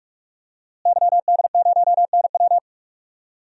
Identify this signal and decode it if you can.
Morse code, heard as a keyed tone, about 46 wpm, 700 Hz. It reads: YD0NW
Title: DXCC-100 Call Signs - 5 Characters - Repeated 3x - 45wpm